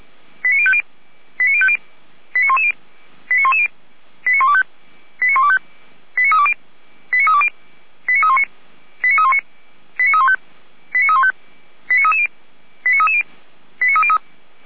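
Five-tone selective call (ZVEI), heard as quick runs of five tones, tones of about 70 ms. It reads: 89050 89050 89110 89110 89125 89125 89320 89320 89319 89319 89315 89315 89330 89330 89303
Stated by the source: ZVEI five-tone selective call 89140 received by SDR, fire brigade Orlamünde (feuerwehr-orlamuende).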